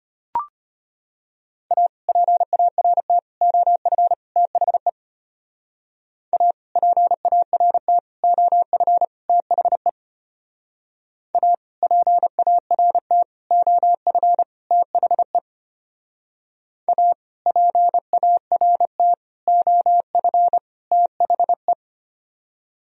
Morse code, heard as a keyed tone, about 30 wpm, 700 Hz. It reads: A PART OF THE A PART OF THE A PART OF THE A PART OF THE
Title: Speed-Racing - Sets of 4 Words 25wpm